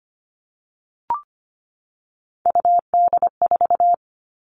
Morse code, heard as a keyed tone, about 25 wpm, 700 Hz. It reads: UD4
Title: All Cut Numbers - 3-Digit 25wpm